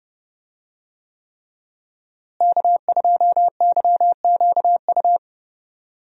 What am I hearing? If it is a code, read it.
Morse code, heard as a keyed tone, about 30 wpm, 700 Hz. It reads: K2YQU
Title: USA Call Signs - 5 Characters - Repeated 3x - 30wpm